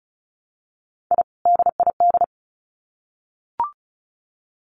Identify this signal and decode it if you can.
Morse code, heard as a keyed tone, about 34 wpm, 700 Hz. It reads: I DID
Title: Sets of 2 Words 35wpm